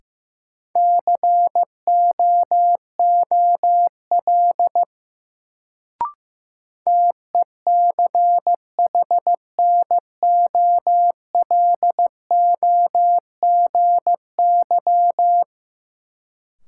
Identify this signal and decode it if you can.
Morse code, heard as a keyed tone, about 15 wpm, 700 Hz. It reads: COOL TECHNOLOGY
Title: Sentences from Top 1200 Words - New Word Drill Alt - 15wpm